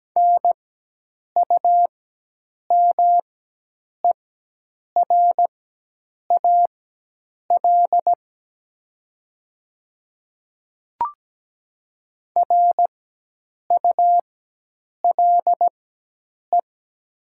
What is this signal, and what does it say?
Morse code, heard as a keyed tone, about 17 wpm, spaced as for 8 wpm, 700 Hz. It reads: NUMERAL RULE
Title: 4x Character Spacing - Top 500 Words - 17wpm